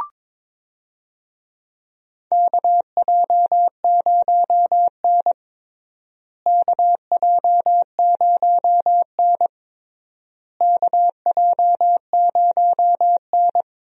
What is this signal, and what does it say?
Morse code, heard as a keyed tone, about 22 wpm, 700 Hz. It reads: KJ0N KJ0N KJ0N
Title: USA Call Signs - 4 Characters - Repeated 3x - 22wpm